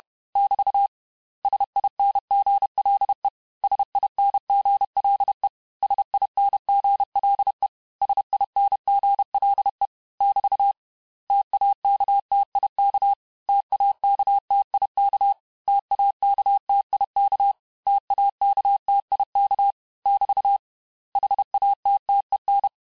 Morse code, heard as a keyed tone, about 31 wpm, 800 Hz. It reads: <BT> SINGLE SINGLE SINGLE SINGLE <BT> TAKTIK TAKTIK TAKTIK TAKTIK <BT> HATTEN
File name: cours-du-mercredi / audios / cours-861-texte-les-mots-vitesse-29MPM